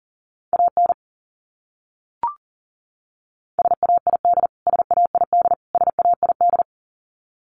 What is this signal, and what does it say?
Morse code, heard as a keyed tone, about 40 wpm, 700 Hz. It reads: AN SAID SAID SAID
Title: Top 500 Words Repeated 40wpm